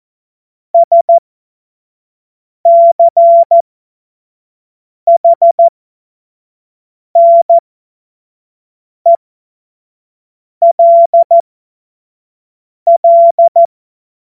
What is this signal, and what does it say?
Morse code, heard as a keyed tone, about 14 wpm, spaced as for 5 wpm, 680 Hz. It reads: SCHNELL
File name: Morsekurs / Witze / Witz 10.Oktober 2023 25 F70